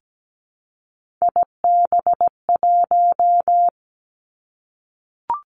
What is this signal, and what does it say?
Morse code, heard as a keyed tone, about 17 wpm, 700 Hz. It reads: IB1
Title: Three Challenging Letter-Number Combinations 17wpm